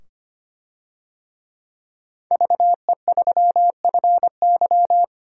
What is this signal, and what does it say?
Morse code, heard as a keyed tone, about 25 wpm, 700 Hz. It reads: VE3FY
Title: DXCC-100 Call Signs - 5 Characters - 25wpm